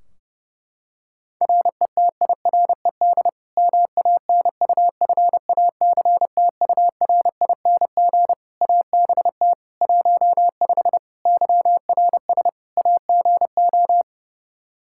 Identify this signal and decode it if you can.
Morse code, heard as a keyed tone, about 32 wpm, 700 Hz. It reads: RETIRED MANUFACTURING ABT 15 YRS AGO
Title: Practice QSO - Exchange 3 - More Info - 30wpm